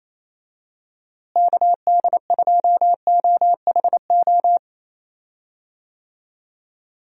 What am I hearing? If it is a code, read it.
Morse code, heard as a keyed tone, about 28 wpm, 700 Hz. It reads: KD2OHO
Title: USA Call Signs - 6 Characters - Repeated 2x - 28wpm